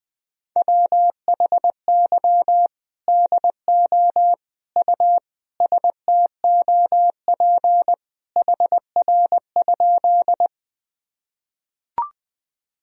Morse code, heard as a keyed tone, about 20 wpm, 700 Hz. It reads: WHY DO U STOP HR?
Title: Sentences from Top 300 Words and Abbreviations 20wpm